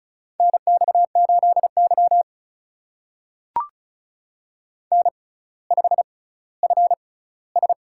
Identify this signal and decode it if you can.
Morse code, heard as a keyed tone, about 36 wpm, spaced as for 18 wpm, 700 Hz. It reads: NX8Y N5FS
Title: USA Call Signs - 4 Characters - 6x Character Spacing - 35wpm